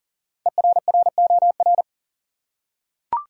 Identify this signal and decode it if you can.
Morse code, heard as a keyed tone, about 40 wpm, 700 Hz. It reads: ERROR
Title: Five Letter Words - 40wpm